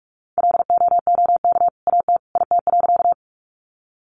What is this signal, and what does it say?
Morse code, heard as a keyed tone, about 44 wpm, 700 Hz. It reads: LOOK AT IT.
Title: Sentences from Top 100 Words 45wpm